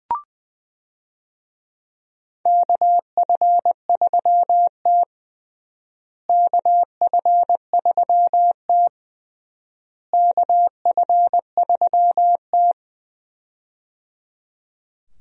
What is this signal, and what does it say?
Morse code, heard as a keyed tone, about 20 wpm, 700 Hz. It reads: KF3T KF3T KF3T
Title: USA Call Signs - 4 Characters - Repeated 3x - 20wpm